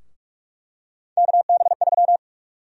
Morse code, heard as a keyed tone, about 42 wpm, 700 Hz. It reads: KB3
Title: USA Call Sign Prefixes - 45wpm